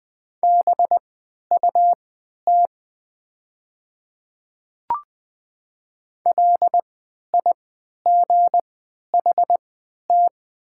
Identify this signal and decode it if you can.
Morse code, heard as a keyed tone, about 20 wpm, spaced as for 11 wpm, 700 Hz. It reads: BUT LIGHT
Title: 3x Character Spacing - Top 200 Words - 20wpm